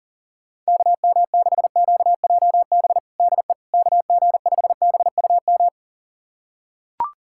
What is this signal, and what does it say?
Morse code, heard as a keyed tone, about 40 wpm, 700 Hz. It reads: KM6QJB DE KG5BUM
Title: USA Callsign DE Callsign - 6 Characters - 40wpm